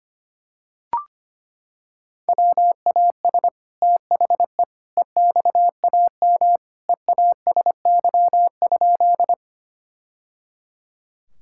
Morse code, heard as a keyed tone, about 25 wpm, 700 Hz. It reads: WAS THE EXAM EASY?
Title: Sentences from Top 1000 Words - Review Part 5 - 25wpm